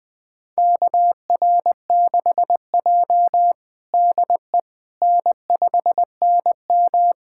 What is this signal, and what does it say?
Morse code, heard as a keyed tone, about 21 wpm, 700 Hz. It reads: KR6J DE N5NM
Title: USA Callsign DE Callsign - 4 Characters - 20wpm